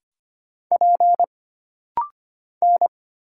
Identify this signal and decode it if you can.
Morse code, heard as a keyed tone, about 25 wpm, 700 Hz. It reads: P N
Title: Single Letter-Number Alt - Rapid-Fire 25wpm